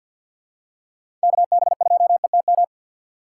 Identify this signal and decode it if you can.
Morse code, heard as a keyed tone, about 47 wpm, 700 Hz. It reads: KB2ETM
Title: USA Call Signs - 6 Characters - 6x Character Spacing - 50wpm